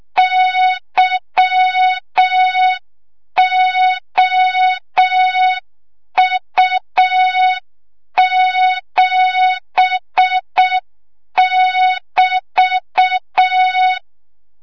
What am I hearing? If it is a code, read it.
Morse code, heard as a keyed tone, about 6 wpm, 3700 Hz. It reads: YOU7<BT>
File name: Morze-HA2MN / hang / 25 GYAKORLAT